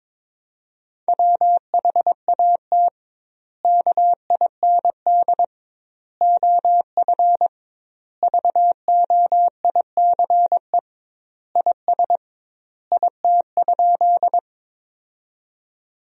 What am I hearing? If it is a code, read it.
Morse code, heard as a keyed tone, about 22 wpm, 700 Hz. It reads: WHAT KIND OF VOICE IS IT?
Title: 2x Word-Spacing - Sentences from Top 500 Words - 22wpm